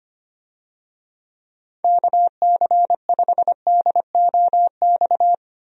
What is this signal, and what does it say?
Morse code, heard as a keyed tone, about 25 wpm, 700 Hz. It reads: KC5DOX
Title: USA Call Signs - 6 Characters - 25wpm